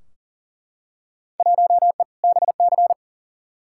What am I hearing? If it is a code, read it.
Morse code, heard as a keyed tone, about 40 wpm, 700 Hz. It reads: 1E BC
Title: Field Day Exchanges - 40wpm